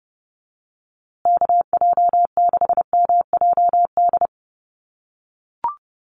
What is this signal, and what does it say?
Morse code, heard as a keyed tone, about 30 wpm, 700 Hz. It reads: KJ6MJD
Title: USA Call Signs Repeated 3x 30wpm